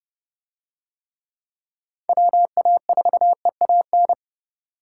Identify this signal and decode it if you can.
Morse code, heard as a keyed tone, about 28 wpm, 700 Hz. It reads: WA4EAN